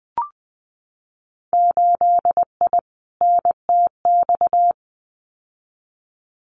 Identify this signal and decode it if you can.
Morse code, heard as a keyed tone, about 20 wpm, 700 Hz. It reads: 8I NTX
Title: Winter Field Day Exchanges Encore 20wpm